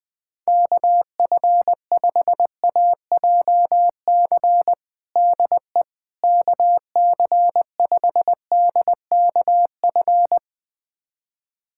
Morse code, heard as a keyed tone, about 20 wpm, 700 Hz. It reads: KF5AJC DE KC5DKF
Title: USA Callsign DE Callsign - 6 Characters - 20wpm